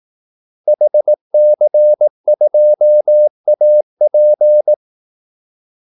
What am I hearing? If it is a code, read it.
Morse code, heard as a keyed tone, about 18 wpm, 600 Hz. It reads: HC2AP
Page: Kaldesignaler 18 wpm